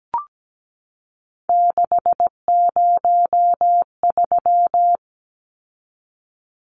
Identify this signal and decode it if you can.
Morse code, heard as a keyed tone, about 17 wpm, 700 Hz. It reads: 603
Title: Numbers - 3-Digit 17wpm